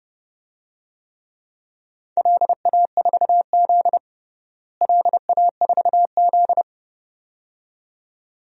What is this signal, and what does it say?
Morse code, heard as a keyed tone, about 30 wpm, 700 Hz. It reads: LA4Z LA4Z
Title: DXCC-100 Call Signs - 4 Characters - Repeated 2x - 30wpm